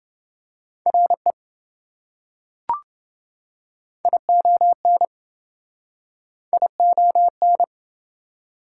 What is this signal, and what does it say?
Morse code, heard as a keyed tone, about 27 wpm, 700 Hz. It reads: RE ION ION